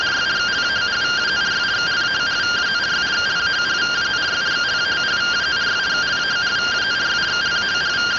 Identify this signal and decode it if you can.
Two-tone teleprinter signal, about 170 Hz shift.
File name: findings / CT AXIAL -142.gen-21 rtty rx